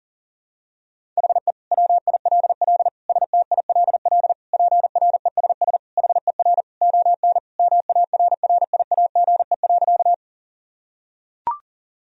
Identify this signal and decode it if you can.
Morse code, heard as a keyed tone, about 40 wpm, 700 Hz. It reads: HE WILL STILL PRESS HER ON MARRIAGE.
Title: Sentences from Top 1000 Words - Review Part 1 - 40wpm